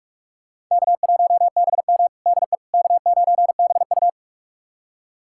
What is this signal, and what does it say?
Morse code, heard as a keyed tone, about 43 wpm, 700 Hz. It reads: K1BM DE K9BU
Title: USA Callsign DE Callsign - 4 Characters - 45wpm